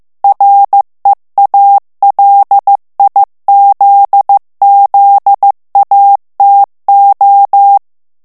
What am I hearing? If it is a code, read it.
Morse code, heard as a keyed tone, about 15 wpm, 800 Hz. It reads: REALIZZATO
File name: morse mp3 / ITALIEN texte clair 2 vitesse 14 mots minute